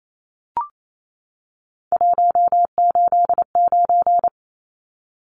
Morse code, heard as a keyed tone, about 28 wpm, 700 Hz. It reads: 189
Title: Numbers - 3-Digit 28wpm